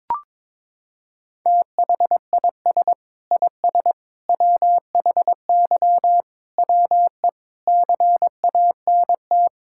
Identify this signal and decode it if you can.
Morse code, heard as a keyed tone, about 22 wpm, 700 Hz. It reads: THIS IS WHY WE CANT
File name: practice / 22wpm / Sets of 8 Words - 22wpm